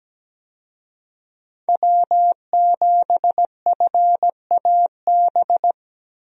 Morse code, heard as a keyed tone, about 17 wpm, 700 Hz. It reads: W7FAB
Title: USA Call Signs - 5 Characters - Repeated 3x - 17wpm